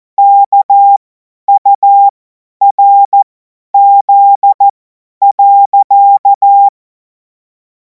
Morse code, heard as a keyed tone, about 14 wpm, spaced as for 10 wpm, 800 Hz. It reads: KURZ.
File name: Morsekurs / Witze / Witz 14. Juli 2026 50 F70